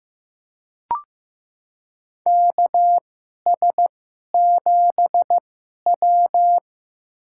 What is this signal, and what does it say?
Morse code, heard as a keyed tone, about 15 wpm, spaced as for 11 wpm, 700 Hz. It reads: KS7W
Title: USA Call Signs - 4 Characters - 2x Character Spacing - 15wpm